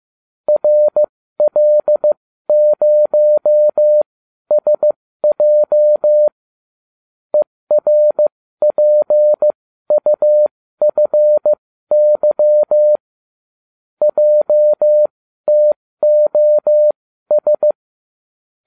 Morse code, heard as a keyed tone, about 15 wpm, spaced as for 12 wpm, 600 Hz. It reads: RL0SJ ERPUFY JTOS